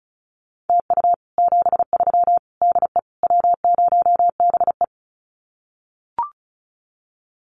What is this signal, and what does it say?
Morse code, heard as a keyed tone, about 34 wpm, 700 Hz. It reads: TU 73 DE W0BE
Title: Common QSO Phrases Encore 35wpm